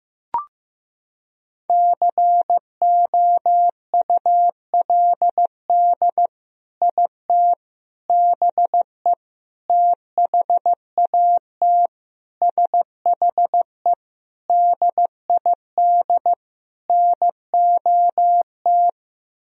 Morse code, heard as a keyed tone, about 16 wpm, 700 Hz. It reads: COULD IT BE THAT SHE DID NOT
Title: Sentences from Top 1000 Words - Review Part 1 - 15wpm